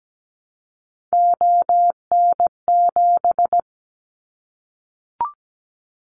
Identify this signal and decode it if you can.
Morse code, heard as a keyed tone, about 17 wpm, 700 Hz. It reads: ON7
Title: Callsign Prefixes for Top 100 DXCC - 17wpm